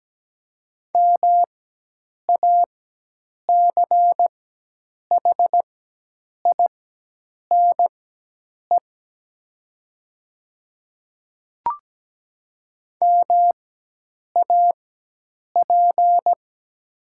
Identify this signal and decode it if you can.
Morse code, heard as a keyed tone, about 17 wpm, spaced as for 8 wpm, 700 Hz. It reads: MACHINE MAP
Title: 4x Character Spacing - Top 500 Words - 17wpm